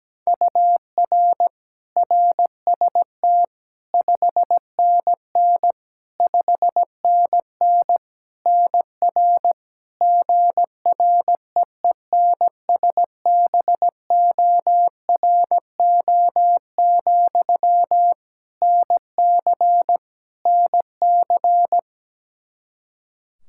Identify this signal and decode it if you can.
Morse code, heard as a keyed tone, about 17 wpm, 700 Hz. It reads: UR RST 5NN 5NN NR GREENSBORO, NC NC
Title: Practice QSO - Exchange 1 - Beginning - 17wpm